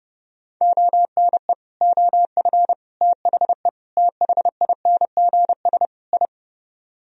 Morse code, heard as a keyed tone, about 37 wpm, 700 Hz. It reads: ONE OF THE THINGS I
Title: Speed-Racing - Sets of 5 Words 30wpm